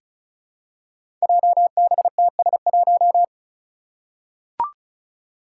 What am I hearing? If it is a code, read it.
Morse code, heard as a keyed tone, about 36 wpm, 700 Hz. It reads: JBTS1
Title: Five Challenging Letter-Number Combinations 35wpm